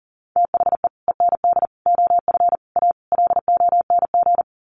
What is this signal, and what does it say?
Morse code, heard as a keyed tone, about 40 wpm, 700 Hz. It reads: THE END OF A LONG
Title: Sets of 5 Words Encore2 40wpm